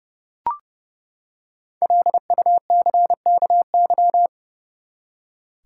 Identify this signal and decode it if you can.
Morse code, heard as a keyed tone, about 30 wpm, 700 Hz. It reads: LUCKY
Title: Five Letter Words - 30wpm